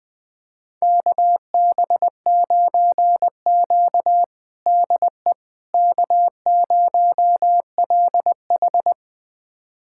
Morse code, heard as a keyed tone, about 20 wpm, 700 Hz. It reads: KB9Q DE K0LH
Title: USA Callsign DE Callsign - 4 Characters - 20wpm